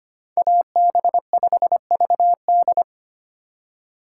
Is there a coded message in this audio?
Yes: Morse code from 0.4 to 2.8 s, heard as a keyed tone, about 33 wpm, 700 Hz. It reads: AB5VD